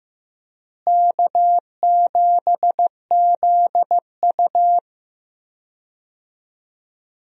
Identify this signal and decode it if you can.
Morse code, heard as a keyed tone, about 15 wpm, 700 Hz. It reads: K7ZU
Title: USA Call Signs - 4 Characters - Repeated 2x - 15wpm